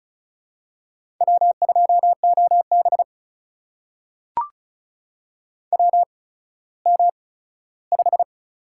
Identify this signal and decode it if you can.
Morse code, heard as a keyed tone, about 36 wpm, spaced as for 17 wpm, 700 Hz. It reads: W2OB WM5
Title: USA Call Signs - 4 Characters - 8x Character Spacing - 35wpm